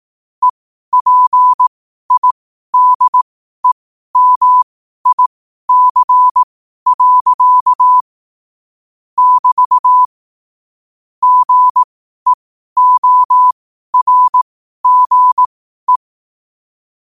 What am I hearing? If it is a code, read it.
Morse code, heard as a keyed tone, about 18 wpm, spaced as for 12 wpm, 1000 Hz. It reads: EPIDEMIC. <BT> GEORGE
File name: qotdfiles / 12wpm